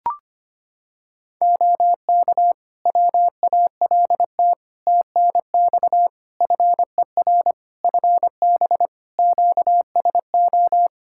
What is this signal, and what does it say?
Morse code, heard as a keyed tone, about 25 wpm, 700 Hz. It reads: OK WALT TNX FER FB QSO
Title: Practice QSO - Exchange 4 - Ending - 25wpm